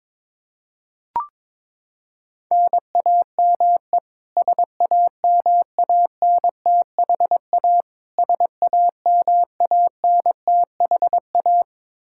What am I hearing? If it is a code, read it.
Morse code, heard as a keyed tone, about 21 wpm, 700 Hz. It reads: NAME SAMANTHA SAMANTHA
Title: Practice QSO - Exchange 1 - Beginning - 22wpm